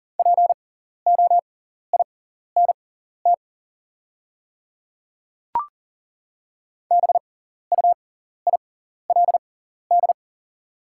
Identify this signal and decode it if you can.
Morse code, heard as a keyed tone, about 40 wpm, spaced as for 14 wpm, 700 Hz. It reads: POINT BUILD
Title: 6x Character Spacing - Top 200 Words - 40wpm